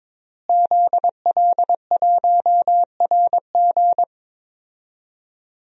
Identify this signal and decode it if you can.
Morse code, heard as a keyed tone, about 21 wpm, 700 Hz. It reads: ZL1RG
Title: DXCC-100 Call Signs - 5 Characters - Repeated 2x - 22wpm